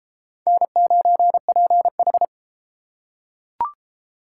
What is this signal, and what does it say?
Morse code, heard as a keyed tone, about 32 wpm, 700 Hz. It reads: N9PH